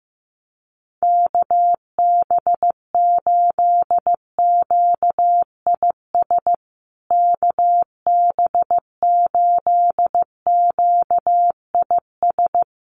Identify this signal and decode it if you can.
Morse code, heard as a keyed tone, about 15 wpm, 700 Hz. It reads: KB8QIS KB8QIS
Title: USA Call Signs Repeated 3x 15wpm